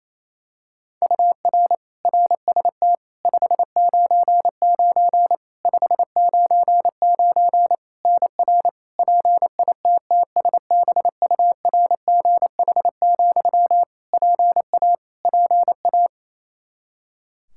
Morse code, heard as a keyed tone, about 28 wpm, 700 Hz. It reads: UR RST 599 599 NR PITTSBURGH, PA PA
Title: Practice QSO - Exchange 1 - Beginning - 28wpm